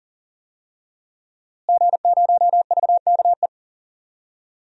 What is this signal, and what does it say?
Morse code, heard as a keyed tone, about 38 wpm, 700 Hz. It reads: G0VKE
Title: DXCC-100 Call Signs - 5 Characters - Repeated 3x - 40wpm